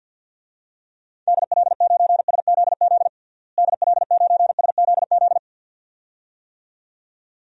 Morse code, heard as a keyed tone, about 51 wpm, 700 Hz. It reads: DL9SZZ DL9SZZ
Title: DXCC-100 Call Signs - 6 Characters - Repeated 2x - 50wpm